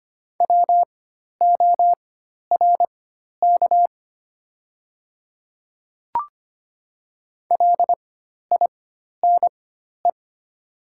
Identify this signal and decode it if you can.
Morse code, heard as a keyed tone, about 25 wpm, spaced as for 12 wpm, 700 Hz. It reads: WORK LINE
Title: 4x Character Spacing - Top 200 Words - 25wpm